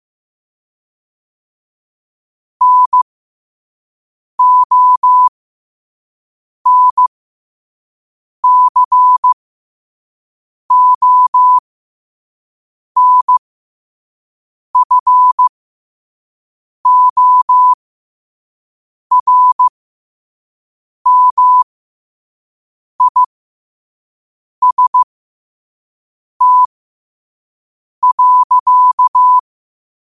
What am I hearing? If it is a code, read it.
Morse code, heard as a keyed tone, about 15 wpm, spaced as for 5 wpm, 1000 Hz. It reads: NONCONFORMIST.
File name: qotdfiles / 5wpm